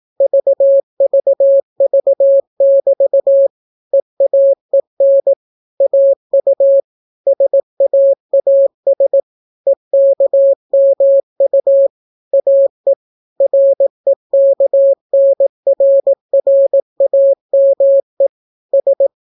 Morse code, heard as a keyed tone, about 18 wpm, 550 Hz. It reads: VVV<BT> EAEN AU SAAS EKMU AE REKNRRAME S